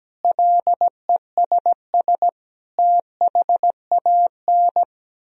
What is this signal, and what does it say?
Morse code, heard as a keyed tone, about 17 wpm, 700 Hz. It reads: LESS THAN
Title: Sentences from Top 1000 Words - Review Part 2 - 17wpm